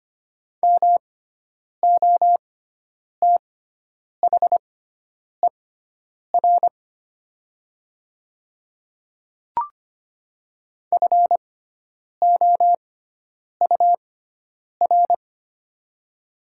Morse code, heard as a keyed tone, about 25 wpm, spaced as for 9 wpm, 700 Hz. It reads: MOTHER FOUR
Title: 6x Character Spacing - Top 300 Words - 25wpm